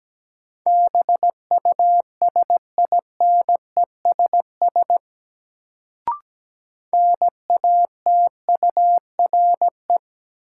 Morse code, heard as a keyed tone, about 17 wpm, 700 Hz. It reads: BUSINESS NATURE
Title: Sentences from Top 600 Words - New Word Drill - 17wpm